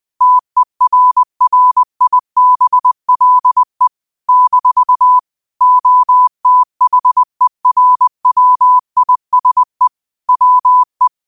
Morse code, heard as a keyed tone, about 20 wpm, 1000 Hz. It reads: TERRIBLE - OTHERWISE WE
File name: qotdfiles / 20wpm